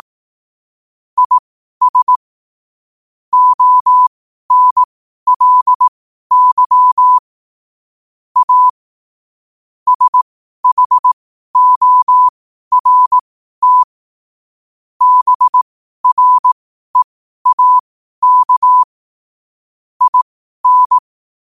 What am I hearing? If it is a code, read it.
Morse code, heard as a keyed tone, about 18 wpm, spaced as for 12 wpm, 1000 Hz. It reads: IS ONLY A SHORT BREAK IN
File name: qotdfiles / 12wpm